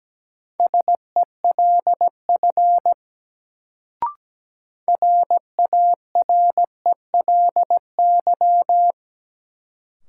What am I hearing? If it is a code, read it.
Morse code, heard as a keyed tone, about 17 wpm, 700 Hz. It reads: SELF RARELY
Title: Sentences from Top 1400 Words - New Word Drill Alt - 17wpm